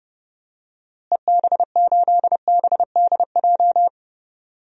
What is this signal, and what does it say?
Morse code, heard as a keyed tone, about 30 wpm, 700 Hz. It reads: EB8BDJ
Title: DXCC-100 Call Signs - 6 Characters - Repeated 3x - 30wpm